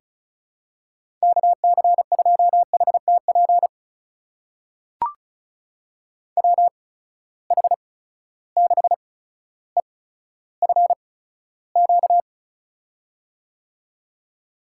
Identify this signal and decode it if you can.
Morse code, heard as a keyed tone, about 36 wpm, spaced as for 15 wpm, 700 Hz. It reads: KC2HTP WH6EFQ
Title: USA Call Signs - 6 Characters - 8x Character Spacing - 35wpm